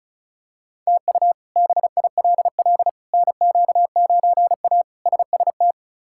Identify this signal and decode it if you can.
Morse code, heard as a keyed tone, about 36 wpm, 700 Hz. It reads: TU BILL NQ9A SST
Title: K1USN SST QSO Practice - 35wpm